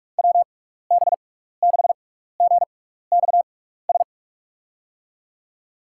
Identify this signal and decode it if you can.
Morse code, heard as a keyed tone, about 46 wpm, spaced as for 15 wpm, 700 Hz. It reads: WB6GXS
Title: USA Call Signs - 6 Characters - 6x Character Spacing - 45wpm